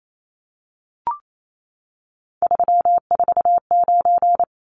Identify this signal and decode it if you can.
Morse code, heard as a keyed tone, about 28 wpm, 700 Hz. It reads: 349
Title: Numbers - 3-Digit 28wpm